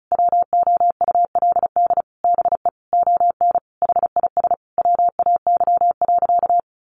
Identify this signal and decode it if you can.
Morse code, heard as a keyed tone, about 34 wpm, 700 Hz. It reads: WOULD BE ON HIS WAY.
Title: Sentences from Top 300 Words 35wpm